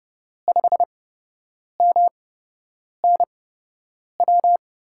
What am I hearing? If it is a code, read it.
Morse code, heard as a keyed tone, about 32 wpm, spaced as for 8 wpm, 700 Hz. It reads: 5MNW
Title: USA Call Signs - 6 Characters - 8x Character Spacing - 30wpm